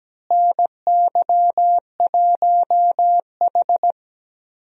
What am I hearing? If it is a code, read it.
Morse code, heard as a keyed tone, about 17 wpm, 700 Hz. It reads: NY1H
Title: USA Call Signs - 4 Characters - Repeated 2x - 17wpm